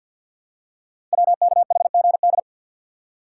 Morse code, heard as a keyed tone, about 52 wpm, 700 Hz. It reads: WKHGD
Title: Code Groups - 5 Letters - 50wpm